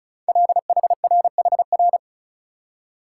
Morse code, heard as a keyed tone, about 36 wpm, 700 Hz. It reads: LHRHR
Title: Code Groups - 5 Letter:Numbers - 35wpm